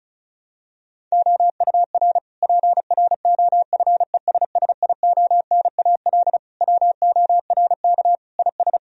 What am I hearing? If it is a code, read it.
Morse code, heard as a keyed tone, about 36 wpm, 700 Hz. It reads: OUR PROFESSIONAL WORK IS
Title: Sentences from Top 1000 Words - Review Part 4 - 35wpm